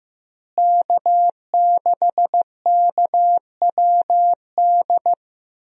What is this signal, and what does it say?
Morse code, heard as a keyed tone, about 15 wpm, 700 Hz. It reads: K6KWD
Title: USA Call Signs - 5 Characters - Repeated 3x - 15wpm